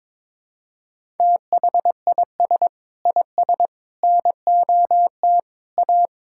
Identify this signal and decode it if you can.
Morse code, heard as a keyed tone, about 22 wpm, 700 Hz. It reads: THIS IS NOT A
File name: practice / 22wpm / Sentences with 3-Word Phrases - 22wpm